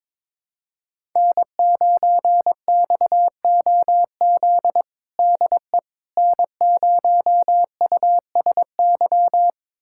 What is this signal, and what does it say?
Morse code, heard as a keyed tone, about 22 wpm, 700 Hz. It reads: N9XOZ DE N0USY
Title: USA Callsign DE Callsign - 5 Characters - 22wpm